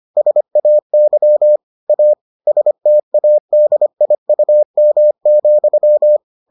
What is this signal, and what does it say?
Morse code, heard as a keyed tone, about 25 wpm, 600 Hz. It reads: SAY A STADIUM,